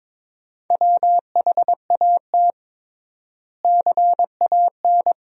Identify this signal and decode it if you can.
Morse code, heard as a keyed tone, about 22 wpm, 700 Hz. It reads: WHAT CAN
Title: 3x Word-Spacing - Sentences from Top 400 Words - 22wpm